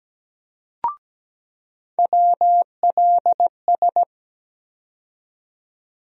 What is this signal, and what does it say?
Morse code, heard as a keyed tone, about 17 wpm, 700 Hz. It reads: WLS
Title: USA Call Sign Postfixes - 3 Letters 17wpm